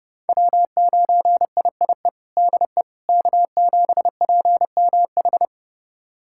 Morse code, heard as a keyed tone, about 32 wpm, 700 Hz. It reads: W9IIE DE K7PMH
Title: USA Callsign DE Callsign - 5 Characters - 30wpm